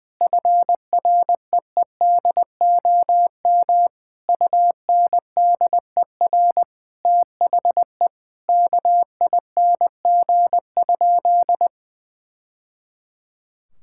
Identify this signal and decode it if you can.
Morse code, heard as a keyed tone, about 21 wpm, 700 Hz. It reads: FREEDOM UNDER THE KING?
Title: Sentences from Top 1000 Words - Review Part 6 - 20wpm